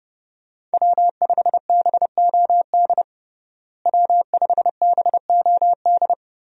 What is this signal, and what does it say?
Morse code, heard as a keyed tone, about 32 wpm, 700 Hz. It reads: W5BOD W5BOD
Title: USA Call Signs - 5 Characters - Repeated 3x - 30wpm